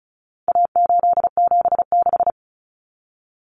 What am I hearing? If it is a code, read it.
Morse code, heard as a keyed tone, about 34 wpm, 700 Hz. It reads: A876
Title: Common Cut Numbers - 4-Digit 35wpm